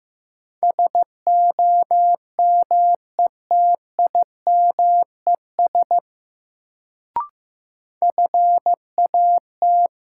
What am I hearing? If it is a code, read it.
Morse code, heard as a keyed tone, about 15 wpm, 700 Hz. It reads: SOMETIMES FAT
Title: Sentences from Top 600 Words - New Word Drill - 15wpm